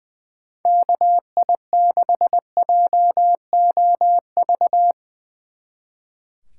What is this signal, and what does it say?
Morse code, heard as a keyed tone, about 23 wpm, 700 Hz. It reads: KI6JOV
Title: Speed Racing - USA Call Signs 20wpm